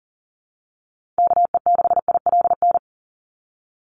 Morse code, heard as a keyed tone, about 40 wpm, 700 Hz. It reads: KE6ILN
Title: USA Call Signs 40wpm